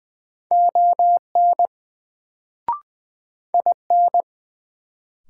Morse code, heard as a keyed tone, about 20 wpm, 700 Hz. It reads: ON IN
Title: Sets of 2 Words - New Word Drill - 20wpm